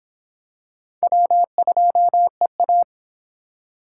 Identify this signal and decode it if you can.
Morse code, heard as a keyed tone, about 26 wpm, 700 Hz. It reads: W2EA